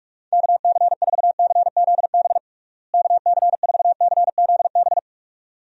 Morse code, heard as a keyed tone, about 46 wpm, 700 Hz. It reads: KC4CZB KC4CZB
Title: USA Call Signs - 6 Characters - Repeated 2x - 45wpm